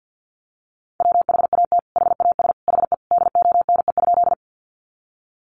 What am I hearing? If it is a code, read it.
Morse code, heard as a keyed tone, about 50 wpm, 700 Hz. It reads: WHAT HAS HE DONE?
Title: Sentences from Top 500 Words 50wpm